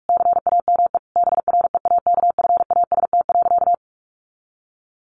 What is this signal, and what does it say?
Morse code, heard as a keyed tone, about 44 wpm, 700 Hz. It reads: CAME BREAKFAST.
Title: Sentences from Top 1500 Words 45wpm